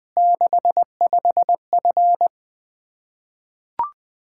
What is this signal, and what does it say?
Morse code, heard as a keyed tone, about 20 wpm, 700 Hz. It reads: 65F
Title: Practice QSO - Exchange 2 - Info - 20wpm